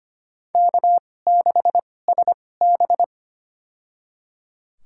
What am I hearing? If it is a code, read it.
Morse code, heard as a keyed tone, about 25 wpm, spaced as for 18 wpm, 700 Hz. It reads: K6SB
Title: USA Call Signs - 4 Characters - 2x Character Spacing - 25wpm